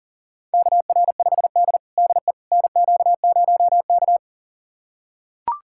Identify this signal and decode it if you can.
Morse code, heard as a keyed tone, about 37 wpm, 700 Hz. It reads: KR5D DE NQ0K
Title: USA Callsign DE Callsign - 4 Characters - 40wpm